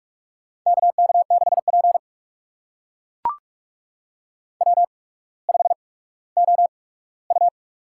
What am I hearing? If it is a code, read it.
Morse code, heard as a keyed tone, about 42 wpm, spaced as for 19 wpm, 700 Hz. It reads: KK6P W5OU
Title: USA Call Signs - 4 Characters - 8x Character Spacing - 45wpm